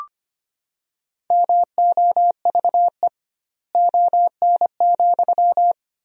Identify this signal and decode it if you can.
Morse code, heard as a keyed tone, about 25 wpm, 700 Hz. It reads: MOVE ON,
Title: 2x Word-Spacing - Sentences from Top 200 Words - 25wpm